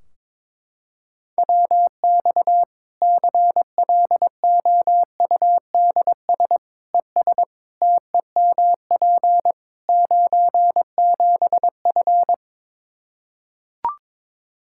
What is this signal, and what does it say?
Morse code, heard as a keyed tone, about 21 wpm, 700 Hz. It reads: WX CLOUDS ES TEMP 97F
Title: Practice QSO - Exchange 2 - Info - 22wpm